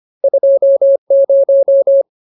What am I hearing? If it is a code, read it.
Morse code, heard as a keyed tone, about 25 wpm, 550 Hz. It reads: 20
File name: Ultimate Word List / 25WPM Ultimate